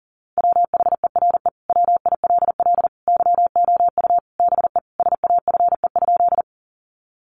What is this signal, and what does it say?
Morse code, heard as a keyed tone, about 40 wpm, 700 Hz. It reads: WHERE WILL YOU BE SAFE?
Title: Sentences from Top 900 Words 40wpm